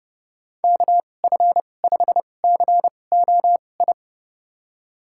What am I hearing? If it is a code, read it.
Morse code, heard as a keyed tone, about 31 wpm, spaced as for 22 wpm, 700 Hz. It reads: KF5COI
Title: USA Call Signs - 6 Characters - 2x Character Spacing - 30wpm